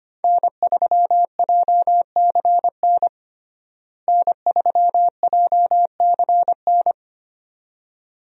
Morse code, heard as a keyed tone, about 25 wpm, 700 Hz. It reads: N3JCN N3JCN
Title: USA Call Signs - 5 Characters - Repeated 2x - 25wpm